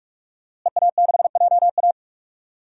Morse code, heard as a keyed tone, about 48 wpm, 700 Hz. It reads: EA6JA
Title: DXCC-100 Call Signs - 5 Characters - Repeated 2x - 45wpm